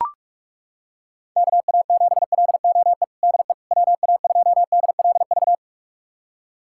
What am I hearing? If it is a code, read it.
Morse code, heard as a keyed tone, about 41 wpm, 700 Hz. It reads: KA7LOE DE WA2DLV
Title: USA Callsign DE Callsign - 6 Characters - 45wpm